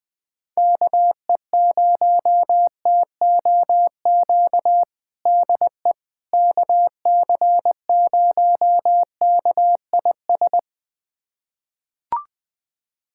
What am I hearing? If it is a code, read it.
Morse code, heard as a keyed tone, about 21 wpm, 700 Hz. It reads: KE0TOQ DE KC0KIS
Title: USA Callsign DE Callsign - 6 Characters - 20wpm